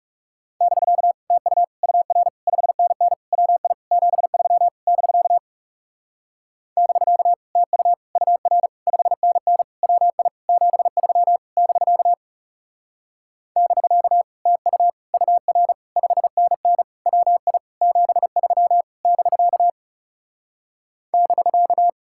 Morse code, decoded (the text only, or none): <BK> TU UR 5NN WI 73 <BK> <BK> TU UR 5NN WI 73 <BK> <BK> TU UR 5NN WI 73 <BK> <BK>